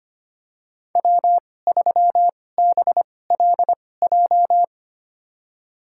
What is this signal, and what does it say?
Morse code, heard as a keyed tone, about 25 wpm, spaced as for 18 wpm, 700 Hz. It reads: W3BLJ
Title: USA Call Signs - 5 Characters - 2x Character Spacing - 25wpm